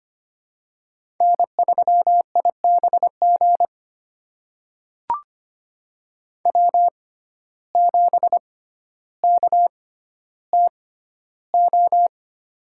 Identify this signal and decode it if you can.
Morse code, heard as a keyed tone, about 25 wpm, spaced as for 13 wpm, 700 Hz. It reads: N3IBG W7KTO
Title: USA Call Signs - 5 Characters - 6x Character Spacing - 25wpm